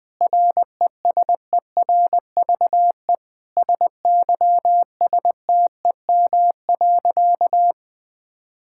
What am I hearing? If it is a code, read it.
Morse code, heard as a keyed tone, about 20 wpm, 700 Hz. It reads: RESERVE SYSTEM.
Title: Sentences from Top 1000 Words - Review Part 6 - 20wpm